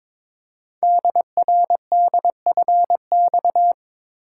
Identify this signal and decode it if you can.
Morse code, heard as a keyed tone, about 22 wpm, 700 Hz. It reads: DRDFX
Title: Code Groups - 5 Letters - 22wpm